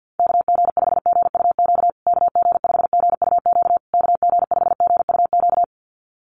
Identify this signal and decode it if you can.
Morse code, heard as a keyed tone, about 50 wpm, 700 Hz. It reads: KG5GUQ KG5GUQ KG5GUQ
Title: USA Call Signs Repeated 3x 50wpm